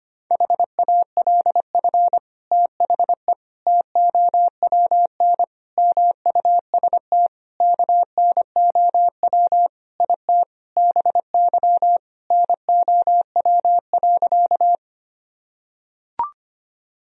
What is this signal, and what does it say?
Morse code, heard as a keyed tone, about 25 wpm, 700 Hz. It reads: HALF THE TOWN MUST KNOW IT BY NOW.
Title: Sentences from Top 1000 Words - Review Part 2 - 25wpm